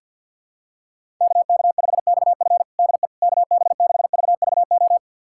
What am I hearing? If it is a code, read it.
Morse code, heard as a keyed tone, about 52 wpm, 700 Hz. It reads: KK5XF DE KB6VVO
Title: USA Callsign DE Callsign 50wpm